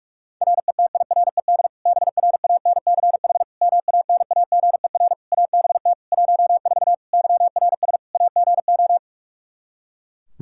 Morse code, heard as a keyed tone, about 46 wpm, 700 Hz. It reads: RETIRED BRANCH MANAGER ABT 14 YRS AGO
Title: Practice QSO - Exchange 3 - More Info - 45wpm